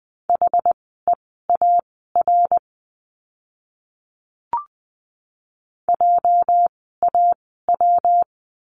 Morse code, heard as a keyed tone, about 20 wpm, spaced as for 15 wpm, 700 Hz. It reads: HEAR JAW
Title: 206 Lesson 33f - Letter j - Words 20wpm